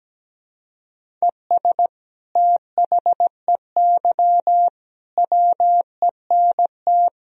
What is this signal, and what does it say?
Morse code, heard as a keyed tone, about 17 wpm, 700 Hz. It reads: ES THEY WENT
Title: Sentences from Top 200 Words and Abbreviations 17wpm